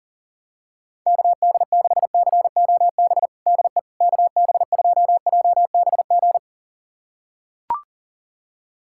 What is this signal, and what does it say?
Morse code, heard as a keyed tone, about 37 wpm, 700 Hz. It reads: KD6COB DE KB2JBG
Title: USA Callsign DE Callsign - 6 Characters - 40wpm